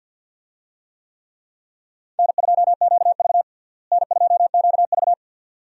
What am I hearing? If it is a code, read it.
Morse code, heard as a keyed tone, about 50 wpm, 700 Hz. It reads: N2QV N2QV
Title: USA Call Signs - 4 Characters - Repeated 2x - 50wpm